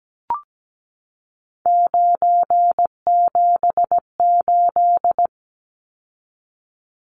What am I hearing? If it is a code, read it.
Morse code, heard as a keyed tone, about 17 wpm, 700 Hz. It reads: 978
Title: Numbers - 3-Digit 17wpm